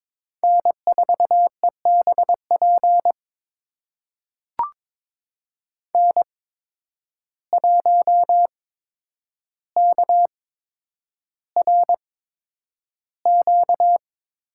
Morse code, heard as a keyed tone, about 21 wpm, spaced as for 9 wpm, 700 Hz. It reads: N4EBP N1KRQ
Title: USA Call Signs - 5 Characters - 8x Character Spacing - 22wpm